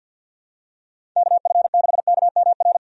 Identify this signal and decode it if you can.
Morse code, heard as a keyed tone, about 46 wpm, 700 Hz. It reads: KF6KMR
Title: USA Call Signs - 6 Characters - 50wpm